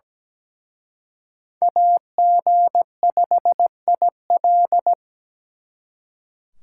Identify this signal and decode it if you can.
Morse code, heard as a keyed tone, about 20 wpm, 700 Hz. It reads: AG5IL
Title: Speed Racing - USA 2x2 Call Signs 17wpm